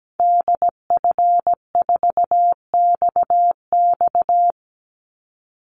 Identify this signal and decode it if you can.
Morse code, heard as a keyed tone, about 17 wpm, 700 Hz. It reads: DF4XX
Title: Call Signs Repeated 3x - 17wpm